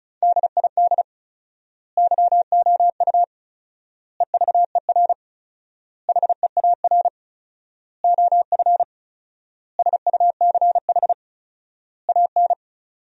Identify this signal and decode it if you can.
Morse code, heard as a keyed tone, about 36 wpm, 700 Hz. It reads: DID YOU EVER HEAR OF SUCH AN
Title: 4x Word-Spacing - Sentences from Top 400 Words - 35wpm